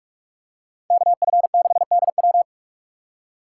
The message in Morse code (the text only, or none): KF6DW